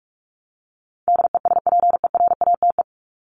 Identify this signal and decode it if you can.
Morse code, heard as a keyed tone, about 44 wpm, 700 Hz. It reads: DESPERATE
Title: Top 1751-2000 Words 45wpm